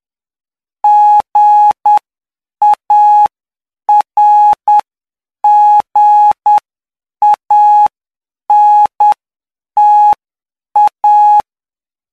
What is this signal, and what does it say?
Morse code, heard as a keyed tone, about 9 wpm, spaced as for 7 wpm, 840 Hz. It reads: GARGANTA